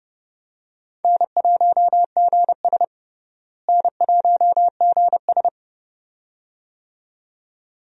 Morse code, heard as a keyed tone, about 29 wpm, 700 Hz. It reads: N1GS N1GS
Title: USA Call Signs - 4 Characters - Repeated 2x - 30wpm